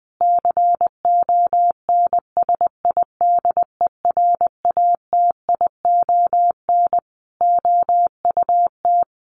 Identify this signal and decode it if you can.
Morse code, heard as a keyed tone, about 20 wpm, 700 Hz. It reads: CONSIDERATION OUT
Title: Sentences from Top 1500 Words 20wpm